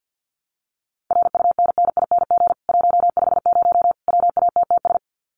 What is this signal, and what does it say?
Morse code, heard as a keyed tone, about 50 wpm, 700 Hz. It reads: RUNNING 150 WATTS
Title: Common QSO Phrases 50wpm